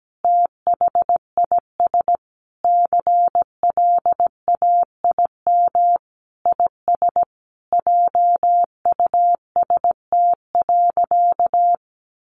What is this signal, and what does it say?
Morse code, heard as a keyed tone, about 17 wpm, 700 Hz. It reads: THIS CLAIM IS JUST.
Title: Sentences from Top 1500 Words 17wpm